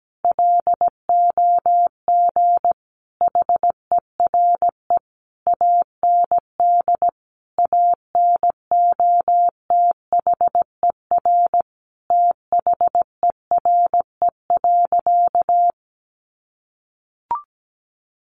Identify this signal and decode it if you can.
Morse code, heard as a keyed tone, about 17 wpm, 700 Hz. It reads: LOG HERE AND ANOTHER THERE.
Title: Sentences from Top 1400 Words 17wpm